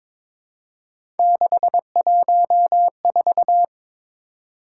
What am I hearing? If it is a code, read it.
Morse code, heard as a keyed tone, about 22 wpm, 700 Hz. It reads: 614
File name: practice / 22wpm / Three Letter-Number Combinations - 22wpm